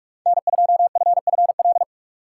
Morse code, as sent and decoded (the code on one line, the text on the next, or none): -. ..--- ..-. ..-. .-..
N2FFL